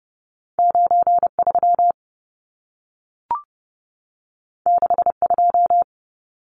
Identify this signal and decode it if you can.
Morse code, heard as a keyed tone, about 30 wpm, 700 Hz. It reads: 93 62
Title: Numbers - 2-Digit 30wpm